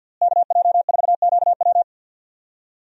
Morse code, heard as a keyed tone, about 48 wpm, 700 Hz. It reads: KJ4QW
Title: USA Call Signs - 5 Characters - Repeated 3x - 50wpm